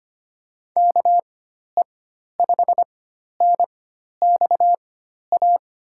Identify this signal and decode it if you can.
Morse code, heard as a keyed tone, about 25 wpm, spaced as for 12 wpm, 700 Hz. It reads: KE5NXA
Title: USA Call Signs - 4x Character Spacing - 25wpm